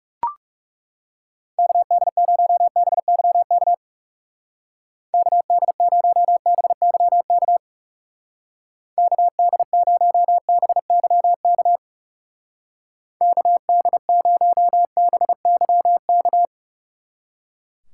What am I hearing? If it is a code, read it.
Morse code, heard as a keyed tone, about 36 wpm, 700 Hz. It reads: KD0BYK KD0BYK KD0BYK KD0BYK
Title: Speed Racing - USA 2x3 Call Signs 30wpm